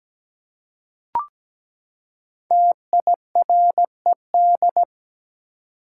Morse code, heard as a keyed tone, about 17 wpm, 700 Hz. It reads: TIRED
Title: Five Letter Words - 17wpm